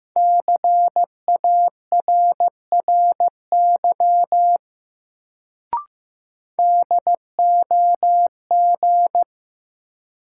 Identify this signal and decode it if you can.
Morse code, heard as a keyed tone, about 15 wpm, 700 Hz. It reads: CARRY DOG
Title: Sentences from Top 400 Words - New Word Drill - 15wpm